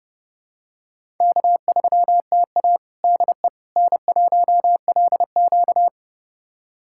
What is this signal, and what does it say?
Morse code, heard as a keyed tone, about 29 wpm, 700 Hz. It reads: K3TA DE N1LQ
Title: USA Callsign DE Callsign - 4 Characters - 30wpm